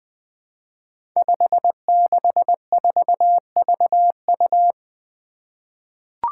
Morse code, heard as a keyed tone, about 20 wpm, 700 Hz. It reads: 564VU
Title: Five Challenging Letter-Number Combinations 20wpm